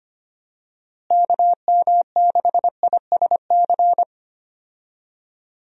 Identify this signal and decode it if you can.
Morse code, heard as a keyed tone, about 25 wpm, 700 Hz. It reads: KM6ISC
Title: USA Call Signs - 6 Characters - Repeated 3x - 25wpm